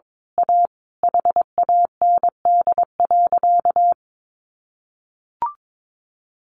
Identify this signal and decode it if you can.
Morse code, heard as a keyed tone, about 22 wpm, 700 Hz. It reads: A HAND.
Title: A Sound Of Thunder 22wpm